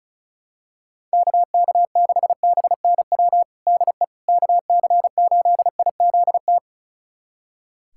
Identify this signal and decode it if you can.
Morse code, heard as a keyed tone, about 36 wpm, 700 Hz. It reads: KK6BNW DE KC8IZT
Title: USA Callsign DE Callsign 35wpm